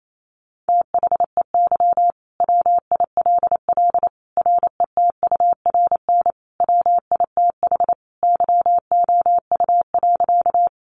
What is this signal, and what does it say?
Morse code, heard as a keyed tone, about 28 wpm, 700 Hz. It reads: THEY WILL RETURN WITH YOU.
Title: Sentences from Top 800 Words 28wpm